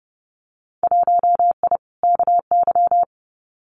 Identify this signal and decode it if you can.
Morse code, heard as a keyed tone, about 30 wpm, 700 Hz. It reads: 1I KY
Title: Winter Field Day Exchanges 30wpm